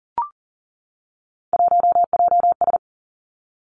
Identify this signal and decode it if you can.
Morse code, heard as a keyed tone, about 38 wpm, 700 Hz. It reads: 1JS
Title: Three Challenging Letter-Number Combinations 40wpm